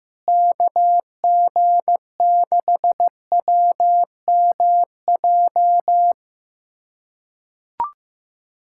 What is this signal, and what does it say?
Morse code, heard as a keyed tone, about 15 wpm, 700 Hz. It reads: KG6WMJ
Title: USA Callsign DE Callsign - 6 Characters - 15wpm